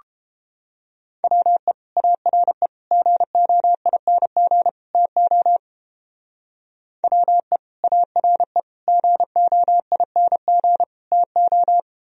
Morse code, heard as a keyed tone, about 29 wpm, 700 Hz. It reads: WE ARE GOING TO WE ARE GOING TO